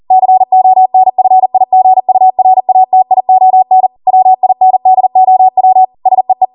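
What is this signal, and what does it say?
Morse code, heard as a keyed tone, about 40 wpm, 750 Hz. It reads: CONFIGURATION WINDOW SEE
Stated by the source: ARRL 40 WPM Morse code practice transmission.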